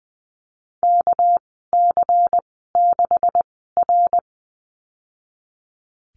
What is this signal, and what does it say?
Morse code, heard as a keyed tone, about 20 wpm, spaced as for 15 wpm, 700 Hz. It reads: KC6R
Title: 207 Lesson 33g - Letter j - Call Signs 20wpm